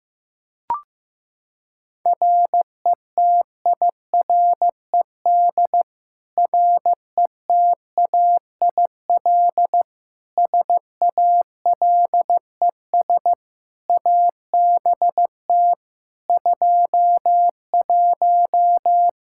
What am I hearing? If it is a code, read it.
Morse code, heard as a keyed tone, about 15 wpm, 700 Hz. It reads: RETIRED RETAIL SALES ABT 21
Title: Practice QSO - Exchange 3 - More Info - 15wpm